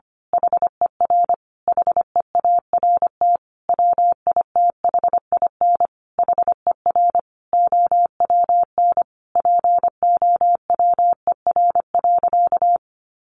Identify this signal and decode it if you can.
Morse code, heard as a keyed tone, about 25 wpm, 700 Hz. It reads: HER HEART WITHIN HER OWN POWER.
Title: Sentences from Top 1200 Words 25wpm